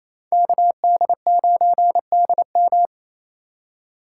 Morse code, heard as a keyed tone, about 36 wpm, 700 Hz. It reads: KD9DM